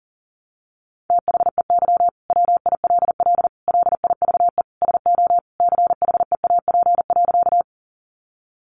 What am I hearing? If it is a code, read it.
Morse code, heard as a keyed tone, about 40 wpm, 700 Hz. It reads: THEY WILL LIVE SO CHEAP.
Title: Sentences from Top 1200 Words 40wpm